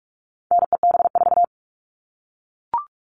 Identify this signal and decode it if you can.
Morse code, heard as a keyed tone, about 44 wpm, 700 Hz. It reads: NEB4
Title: All Cut Numbers - 4-Digit 45wpm